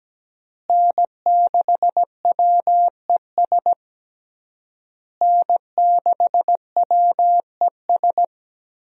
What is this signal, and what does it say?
Morse code, heard as a keyed tone, about 17 wpm, 700 Hz. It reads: N6WES N6WES
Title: USA Call Signs - 5 Characters - Repeated 3x - 17wpm